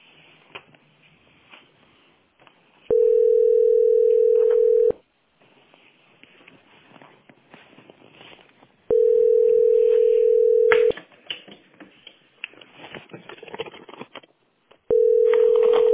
Ringback tone (North America), 440 + 480 Hz.